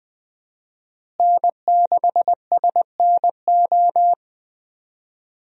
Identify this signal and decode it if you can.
Morse code, heard as a keyed tone, about 20 wpm, 700 Hz. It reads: N6SNO
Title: USA Call Signs - 5 Characters - Repeated 2x - 20wpm